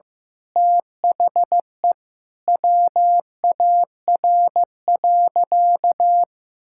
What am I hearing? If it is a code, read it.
Morse code, heard as a keyed tone, about 15 wpm, 700 Hz. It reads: THE WAR.
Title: Sentences from Top 1000 Words - Review Part 6 - 15wpm